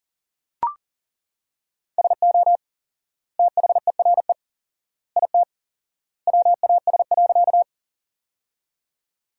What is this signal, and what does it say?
Morse code, heard as a keyed tone, about 40 wpm, 700 Hz. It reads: SO THERE IT WAS.
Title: 4x Word-Spacing - Sentences from Top 100 Words - 40wpm